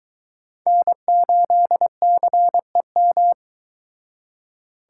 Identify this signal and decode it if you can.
Morse code, heard as a keyed tone, about 22 wpm, 700 Hz. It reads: N8CEM